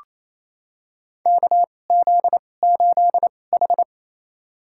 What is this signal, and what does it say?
Morse code, heard as a keyed tone, about 28 wpm, spaced as for 20 wpm, 700 Hz. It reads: KZ8H
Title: USA Call Signs - 4 Characters - 2x Character Spacing - 28wpm